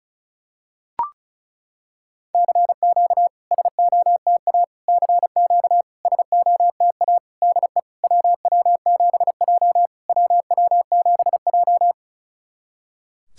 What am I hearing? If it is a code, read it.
Morse code, heard as a keyed tone, about 35 wpm, 690 Hz. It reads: CQ SOTA CQ SOTA DE WW7J WW7J
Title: SOTA QSO Practice 35wpm